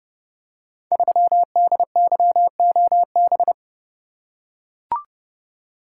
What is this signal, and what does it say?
Morse code, heard as a keyed tone, about 29 wpm, 700 Hz. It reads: 3DYOB
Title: Code Groups - 5 Letter:Numbers - 30wpm